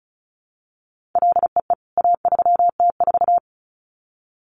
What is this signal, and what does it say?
Morse code, heard as a keyed tone, about 34 wpm, 700 Hz. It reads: LEE A3T4
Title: CWT Exchange 35wpm